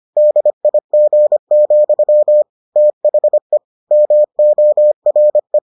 Morse code, heard as a keyed tone, about 25 wpm, 600 Hz. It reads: DIG, THE MORE